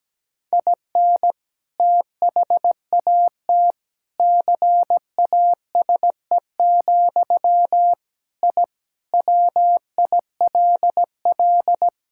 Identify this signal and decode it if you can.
Morse code, heard as a keyed tone, about 17 wpm, 700 Hz. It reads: IN THAT CASE, I WILL
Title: Sentences from Top 1000 Words - Review Part 6 - 17wpm